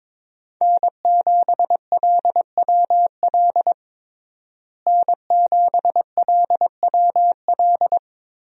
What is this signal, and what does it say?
Morse code, heard as a keyed tone, about 22 wpm, 700 Hz. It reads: N7LWL N7LWL
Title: USA Call Signs - 5 Characters - Repeated 3x - 22wpm